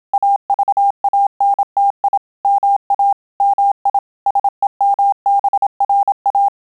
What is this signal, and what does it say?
Morse code, heard as a keyed tone, about 27 wpm, 800 Hz. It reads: AVANTI MA MI SEMBRA